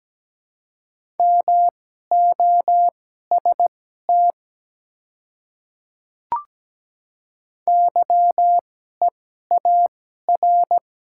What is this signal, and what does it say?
Morse code, heard as a keyed tone, about 17 wpm, spaced as for 12 wpm, 700 Hz. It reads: MOST YEAR
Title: 2x Character Spacing - Top 200 Words - 17wpm